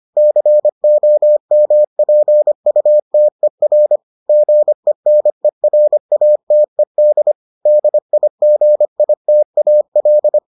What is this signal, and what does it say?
Morse code, heard as a keyed tone, about 25 wpm, 600 Hz. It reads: COMPUTER GENERATED DIGITAL